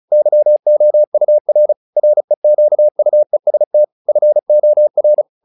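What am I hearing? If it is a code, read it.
Morse code, heard as a keyed tone, about 34 wpm, 600 Hz. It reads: YOUR REQUEST FOR
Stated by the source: Morse practice text "Tower Trouble" sent at 35 WPM, from the SKCC - Straight Key Century Club learning center audio files.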